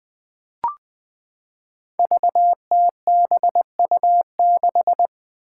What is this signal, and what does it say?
Morse code, heard as a keyed tone, about 20 wpm, 700 Hz. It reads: VTBU6
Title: Five Challenging Letter-Number Combinations 20wpm